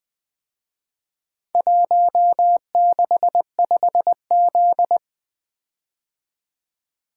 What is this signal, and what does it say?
Morse code, heard as a keyed tone, about 21 wpm, 700 Hz. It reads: 165Z
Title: Speed-Racing - Four Letter-Number Combinations - 20wpm